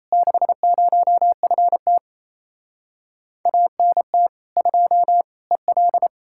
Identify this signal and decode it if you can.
Morse code, heard as a keyed tone, about 37 wpm, 700 Hz. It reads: 60FT ANT 2 EL